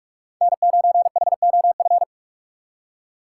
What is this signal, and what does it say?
Morse code, heard as a keyed tone, about 46 wpm, 700 Hz. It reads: N9HOF
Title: USA Call Signs - 2x Character Spacing - 45wpm